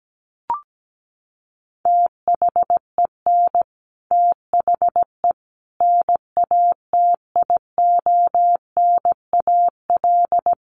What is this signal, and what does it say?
Morse code, heard as a keyed tone, about 17 wpm, 700 Hz. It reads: THEN THE NATIONAL
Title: Sentences from Top 1500 Words 17wpm